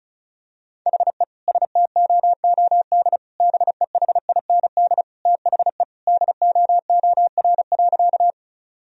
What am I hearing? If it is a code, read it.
Morse code, heard as a keyed tone, about 36 wpm, 700 Hz. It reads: HE STOOD BEHIND THE DOOR.
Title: Sentences from Top 1000 Words - Review Part 2 - 35wpm